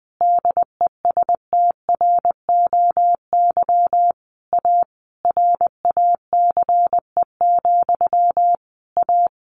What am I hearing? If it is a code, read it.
Morse code, heard as a keyed tone, about 20 wpm, 700 Hz. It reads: DESTROY A RACE, A
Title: A Sound Of Thunder 20wpm